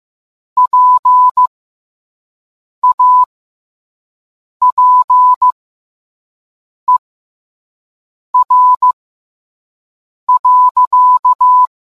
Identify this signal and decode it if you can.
Morse code, heard as a keyed tone, about 15 wpm, spaced as for 5 wpm, 1000 Hz. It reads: PAPER.